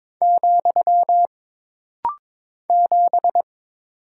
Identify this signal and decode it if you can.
Morse code, heard as a keyed tone, about 22 wpm, 700 Hz. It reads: , 7
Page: This rapid-fire timing is geared to push your ICR (Instant Character Recognition) skills.